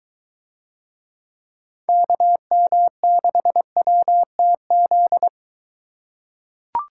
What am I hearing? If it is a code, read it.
Morse code, heard as a keyed tone, about 22 wpm, 700 Hz. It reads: KM6WTZ